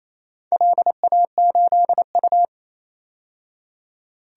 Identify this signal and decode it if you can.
Morse code, heard as a keyed tone, about 28 wpm, 700 Hz. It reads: LA8U
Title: DXCC-100 Call Signs - 4 Characters - 28wpm